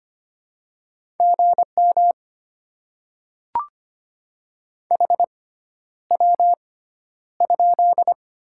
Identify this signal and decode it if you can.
Morse code, heard as a keyed tone, about 25 wpm, spaced as for 11 wpm, 700 Hz. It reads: GM HW?
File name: Top 20 QSO Elements - 6x Character Spacing - 25wpm